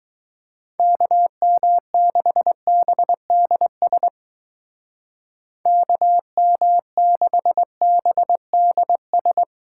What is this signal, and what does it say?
Morse code, heard as a keyed tone, about 21 wpm, 700 Hz. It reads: KM6BDS KM6BDS